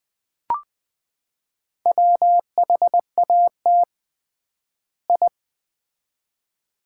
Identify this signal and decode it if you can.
Morse code, heard as a keyed tone, about 20 wpm, 700 Hz. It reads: WHAT I
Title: 3x Word-Spacing - Sets of 2 Words - 20wpm